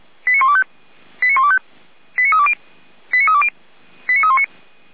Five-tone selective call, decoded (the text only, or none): 89125 89125 89320 89320 89319